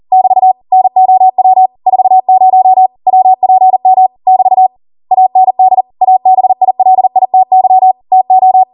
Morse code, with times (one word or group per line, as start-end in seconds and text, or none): 0.1-0.5 <BT>
0.7-1.7 NOW
1.9-2.9 40
3.1-4.1 WPM
4.3-4.7 <BT>
5.1-5.8 AND
6.0-7.9 ABILITY
8.1-8.6 TO